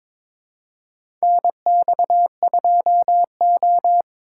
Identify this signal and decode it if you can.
Morse code, heard as a keyed tone, about 22 wpm, 700 Hz. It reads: NX2O
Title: USA Call Signs - 2x Character Spacing - 22wpm